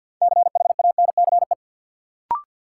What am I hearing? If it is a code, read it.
Morse code, heard as a keyed tone, about 53 wpm, 700 Hz. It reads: CHANCE
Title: Sets of 3 Words - New Word Drill - 50wpm